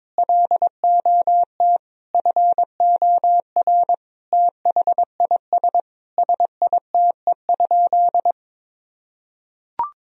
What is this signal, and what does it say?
Morse code, heard as a keyed tone, about 22 wpm, 700 Hz. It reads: LOT FOR THIS SITE?
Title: Sentences from Top 1000 Words - Review Part 4 - 22wpm